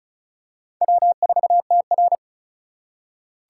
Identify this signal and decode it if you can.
Morse code, heard as a keyed tone, about 36 wpm, 700 Hz. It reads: W4TR
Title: USA Call Signs - 4 Characters - 3x Character Spacing - 35wpm